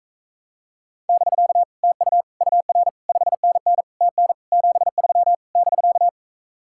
Morse code, heard as a keyed tone, about 38 wpm, 700 Hz. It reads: <BK> TU UR 5NN TN 73 <BK>